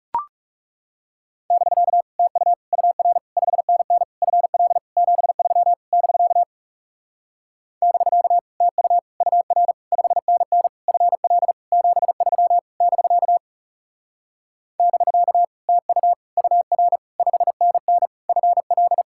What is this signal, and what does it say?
Morse code, heard as a keyed tone, about 40 wpm, 700 Hz. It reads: <BK> TU UR 5NN FL 73 <BK> <BK> TU UR 5NN FL 73 <BK> <BK> TU UR 5NN FL